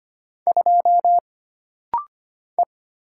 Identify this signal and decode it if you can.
Morse code, heard as a keyed tone, about 25 wpm, 700 Hz. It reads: 2 E
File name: Single Character (Letter:Number:Punctuation:Prosign) - Rapid-Fire - 25wpm